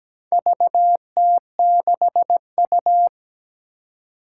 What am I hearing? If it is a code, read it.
Morse code, heard as a keyed tone, about 17 wpm, 700 Hz. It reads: VT6U
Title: Four Challenging Letter-Number Combinations 17wpm